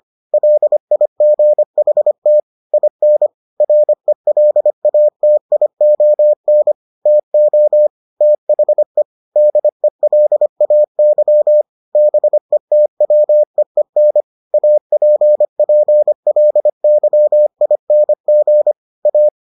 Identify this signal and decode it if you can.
Morse code, heard as a keyed tone, about 25 wpm, 600 Hz. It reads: LIGHT IN RELATION TO THE DELAY BETWEEN APPLYING A